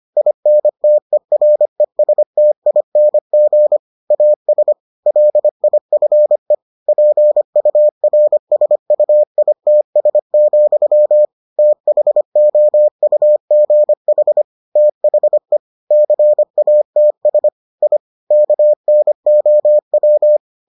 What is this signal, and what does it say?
Morse code, heard as a keyed tone, about 25 wpm, 600 Hz. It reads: INTERESTING AS LIFE PURSUITS, THOUGH THE CATS I KNOW